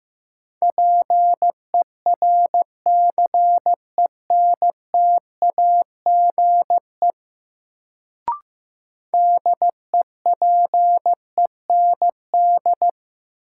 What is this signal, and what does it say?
Morse code, heard as a keyed tone, about 15 wpm, 700 Hz. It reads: PERCENTAGE DEPEND
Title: Sentences from Top 1300 Words - New Word Drill Alt - 15wpm